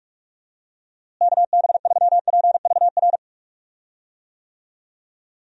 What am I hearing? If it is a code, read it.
Morse code, heard as a keyed tone, about 44 wpm, 700 Hz. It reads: KB3PVR
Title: USA Call Signs - 6 Characters - Repeated 2x - 45wpm